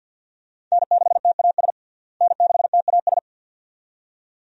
Morse code, heard as a keyed tone, about 51 wpm, 700 Hz. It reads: N6TAS N6TAS
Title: USA Call Signs - 5 Characters - Repeated 2x - 50wpm